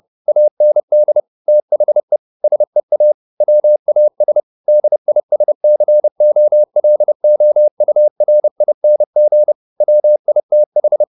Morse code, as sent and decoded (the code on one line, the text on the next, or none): .- -. -.. / - .... / ... .- / .-- .- ... / -.. .. ... -.-. --- .-.. --- ..- .-. .. -. --. / .-- .. - ....
AND THE SEA WAS DISCOLOURING WITH